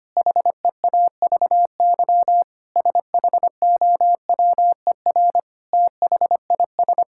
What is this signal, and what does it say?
Morse code, heard as a keyed tone, about 25 wpm, 700 Hz. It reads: HEAVY SHOWER THIS
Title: Sentences from Top 1600 Words 25wpm